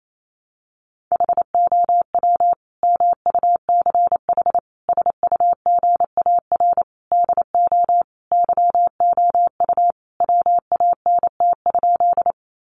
Morse code, heard as a keyed tone, about 28 wpm, 700 Hz. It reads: HOW MUCH SUGAR DO YOU WANT?
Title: Sentences from Top 1200 Words 28wpm